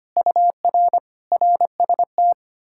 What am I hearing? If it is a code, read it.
Morse code, heard as a keyed tone, about 33 wpm, 700 Hz. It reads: UR RST